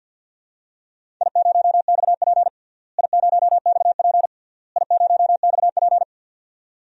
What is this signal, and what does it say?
Morse code, heard as a keyed tone, about 50 wpm, 700 Hz. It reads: I0XP I0XP I0XP
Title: DXCC-100 Call Signs - 4 Characters - Repeated 3x - 50wpm